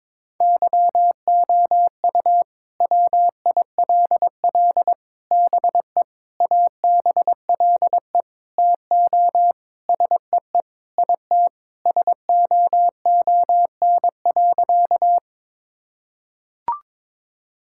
Morse code, heard as a keyed tone, about 22 wpm, 700 Hz. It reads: YOU WILL BE ABLE TO SEE IT SOON.
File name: practice / 22wpm / Sentences with 4-Word Phrases - 22wpm